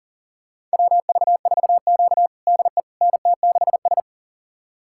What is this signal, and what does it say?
Morse code, heard as a keyed tone, about 37 wpm, 700 Hz. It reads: WV4Q DE NT6S
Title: USA Callsign DE Callsign - 4 Characters - 40wpm